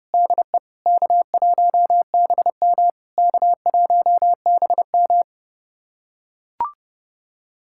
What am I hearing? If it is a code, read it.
Morse code, heard as a keyed tone, about 30 wpm, 700 Hz. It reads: DE K1BM K1BM
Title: SOTA QSO Practice 30wpm